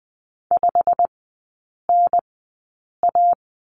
Morse code, heard as a keyed tone, about 20 wpm, spaced as for 8 wpm, 700 Hz. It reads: 5NA
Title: 107 Lesson 18e - Letter m - Three Characters 20wpm